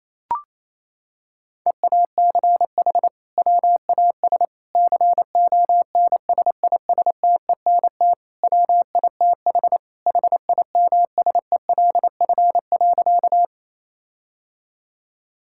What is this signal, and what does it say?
Morse code, heard as a keyed tone, about 28 wpm, 700 Hz. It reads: EACH WAS CONSISTENT WITH HIMSELF.
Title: Sentences from Top 1600 Words 28wpm